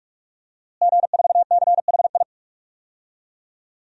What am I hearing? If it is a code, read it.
Morse code, heard as a keyed tone, about 46 wpm, 700 Hz. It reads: G4CHI
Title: DXCC-100 Call Signs - 5 Characters - Repeated 2x - 45wpm